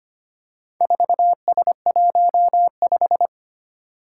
Morse code, heard as a keyed tone, about 25 wpm, 700 Hz. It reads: 4S15
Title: Four Challenging Letter-Number Combinations 25wpm